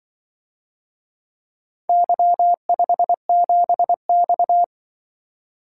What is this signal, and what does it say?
Morse code, heard as a keyed tone, about 25 wpm, 700 Hz. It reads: Y57X
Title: Speed-Racing - Four Letter-Number Combinations - 20wpm